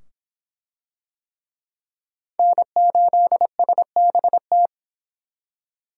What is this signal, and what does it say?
Morse code, heard as a keyed tone, about 25 wpm, 700 Hz. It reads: N8SBT